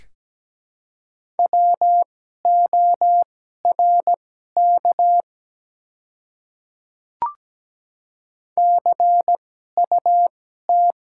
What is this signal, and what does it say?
Morse code, heard as a keyed tone, about 17 wpm, spaced as for 12 wpm, 690 Hz. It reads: WORK CUT
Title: 2x Character Spacing - Top 300 Words - 17wpm